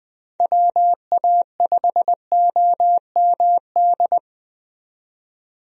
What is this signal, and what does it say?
Morse code, heard as a keyed tone, about 20 wpm, 700 Hz. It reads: WA5OMD
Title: USA Call Signs - 6 Characters - 20wpm